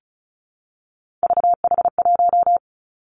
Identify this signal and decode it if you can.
Morse code, heard as a keyed tone, about 36 wpm, 700 Hz. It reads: VH1
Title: Three Challenging Letter-Number Combinations 35wpm